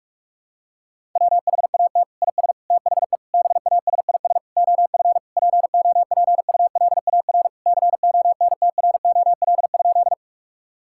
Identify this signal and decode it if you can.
Morse code, heard as a keyed tone, about 41 wpm, 700 Hz. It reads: WHAT IS THE BASIS OF POPULAR CONTROL?
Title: Sentences from Top 1000 Words - Review Part 5 - 45wpm